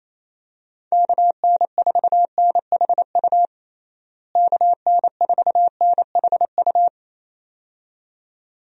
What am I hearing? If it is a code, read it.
Morse code, heard as a keyed tone, about 27 wpm, 700 Hz. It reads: KN4NHU KN4NHU
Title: USA Call Signs - 6 Characters - Repeated 2x - 28wpm